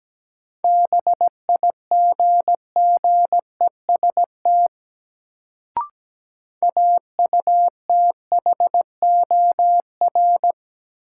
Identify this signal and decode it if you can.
Morse code, heard as a keyed tone, about 17 wpm, 700 Hz. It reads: BIGGEST AUTHOR
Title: Sets of 4 Words - New Word Drill - 17wpm